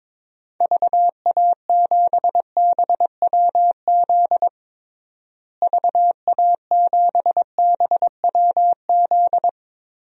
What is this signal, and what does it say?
Morse code, heard as a keyed tone, about 21 wpm, 700 Hz. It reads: VA7BWZ VA7BWZ
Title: DXCC-100 Call Signs - 6 Characters - Repeated 2x - 22wpm